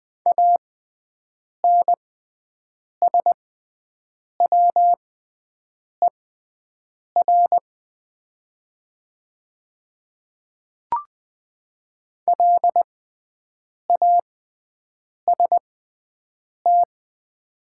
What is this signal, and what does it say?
Morse code, heard as a keyed tone, about 21 wpm, spaced as for 7 wpm, 690 Hz. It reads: ANSWER LAST
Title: 6x Character Spacing - Top 300 Words - 20wpm